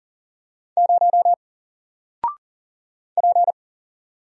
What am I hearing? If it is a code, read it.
Morse code, heard as a keyed tone, about 40 wpm, 700 Hz. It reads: 0 P
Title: Single Letter-Number 40wpm